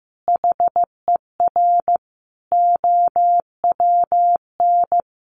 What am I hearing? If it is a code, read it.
Morse code, heard as a keyed tone, about 15 wpm, 700 Hz. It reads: HER OWN
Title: Sentences from Top 1300 Words 15wpm